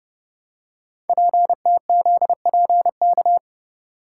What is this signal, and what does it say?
Morse code, heard as a keyed tone, about 30 wpm, 700 Hz. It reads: PTZPK
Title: Code Groups - 5 Letters - 30wpm